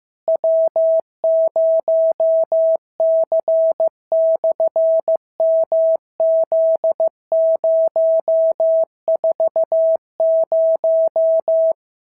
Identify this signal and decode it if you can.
Morse code, heard as a keyed tone, about 15 wpm, 650 Hz. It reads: W0C/MZ040
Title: SOTA QSO Practice 15wpm